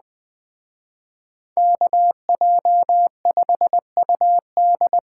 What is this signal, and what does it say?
Morse code, heard as a keyed tone, about 20 wpm, 700 Hz. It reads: KJ5UD
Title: Speed Racing - USA 2x2 Call Signs 17wpm